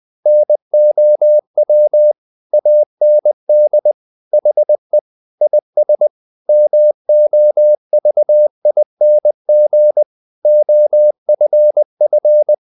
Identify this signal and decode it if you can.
Morse code, heard as a keyed tone, about 20 wpm, 600 Hz. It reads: NOW AND HE IS MOVING OFF
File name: the old man and the sea / the old man and the sea part 1 20wpm